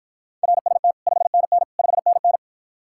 Morse code, heard as a keyed tone, about 53 wpm, 700 Hz. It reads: RST 5NN 5NN